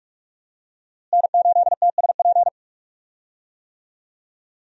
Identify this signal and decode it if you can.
Morse code, heard as a keyed tone, about 46 wpm, 700 Hz. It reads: N8TSP
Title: USA Call Signs - 5 Characters - Repeated 2x - 45wpm